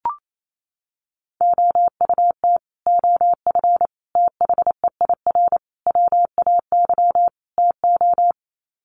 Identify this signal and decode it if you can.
Morse code, heard as a keyed tone, about 28 wpm, 700 Hz. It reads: OUT OF THEIR WAY TO
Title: Sets of 5 Words Encore2 28wpm